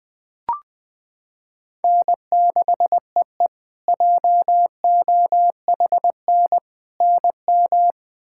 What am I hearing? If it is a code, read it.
Morse code, heard as a keyed tone, about 20 wpm, 700 Hz. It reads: N6EE JOHN NM
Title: K1USN SST QSO Practice - 20wpm